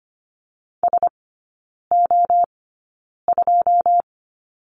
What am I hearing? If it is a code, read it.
Morse code, heard as a keyed tone, about 25 wpm, spaced as for 9 wpm, 700 Hz. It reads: SO2
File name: course / 25:10wpm / 093 Lesson 16e - Letter c - Three Characters 25wpm